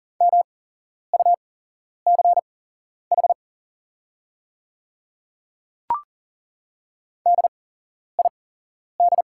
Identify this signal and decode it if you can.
Morse code, heard as a keyed tone, about 40 wpm, spaced as for 11 wpm, 690 Hz. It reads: MUCH DID
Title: 8x Character Spacing - Top 200 Words - 40wpm